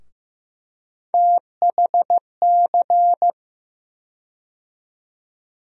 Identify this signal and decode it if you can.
Morse code, heard as a keyed tone, about 16 wpm, 700 Hz. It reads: THC
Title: USA Call Sign Postfixes - 3 Letters 15wpm